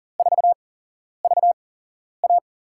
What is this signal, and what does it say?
Morse code, heard as a keyed tone, about 39 wpm, spaced as for 11 wpm, 700 Hz. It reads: 4VA
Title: USA Call Signs - 4 Characters - 8x Character Spacing - 40wpm